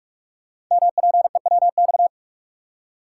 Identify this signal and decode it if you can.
Morse code, heard as a keyed tone, about 44 wpm, 690 Hz. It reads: MPEWX
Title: Code Groups - 5 Letters - 45wpm